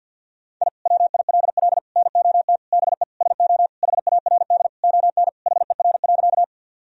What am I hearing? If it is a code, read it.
Morse code, heard as a keyed tone, about 50 wpm, 700 Hz. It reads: I WILL NOT BE SO HARD ON HER.
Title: Sentences from Top 1000 Words - Review Part 1 - 50wpm